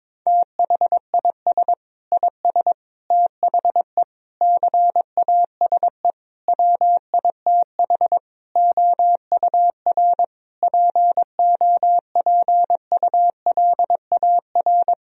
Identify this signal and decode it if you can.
Morse code, heard as a keyed tone, about 22 wpm, 700 Hz. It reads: THIS IS THE CASE WITH OUR POPULAR
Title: Sentences from Top 1000 Words - Review Part 3 - 22wpm